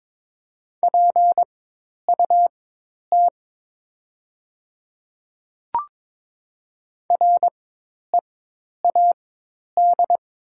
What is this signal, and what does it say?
Morse code, heard as a keyed tone, about 22 wpm, spaced as for 10 wpm, 700 Hz. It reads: PUT READ
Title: 4x Character Spacing - Top 200 Words - 22wpm